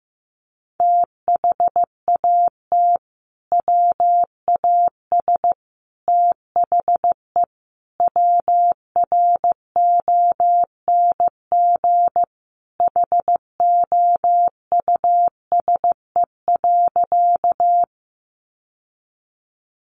Morse code, heard as a keyed tone, about 15 wpm, 700 Hz. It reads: THAT WAS THE WRONG HOUSE.
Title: Sentences from Top 900 Words 15wpm